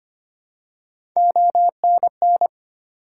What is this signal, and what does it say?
Morse code, heard as a keyed tone, about 25 wpm, 700 Hz. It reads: ONN
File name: practice / 25wpm / ARRL Sections v2 - 25wpm